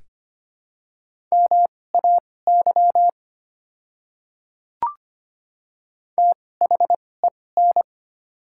Morse code, heard as a keyed tone, about 25 wpm, spaced as for 18 wpm, 700 Hz. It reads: MAY THEN
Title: 2x Character Spacing - Top 100 Words - 25wpm